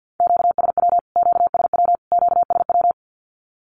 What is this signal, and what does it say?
Morse code, heard as a keyed tone, about 50 wpm, 700 Hz. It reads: QSW QSW QSW
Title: Q-Codes Repeated 3x 50wpm